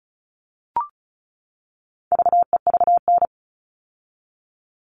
Morse code, heard as a keyed tone, about 34 wpm, 700 Hz. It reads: VEVN
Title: All Cut Numbers - 4-Digit 35wpm